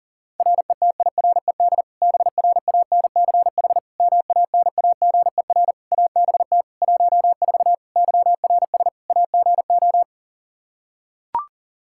Morse code, heard as a keyed tone, about 40 wpm, 700 Hz. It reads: RETIRED BRANCH MANAGER ABT 14 YRS AGO
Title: Practice QSO - Exchange 3 - More Info - 40wpm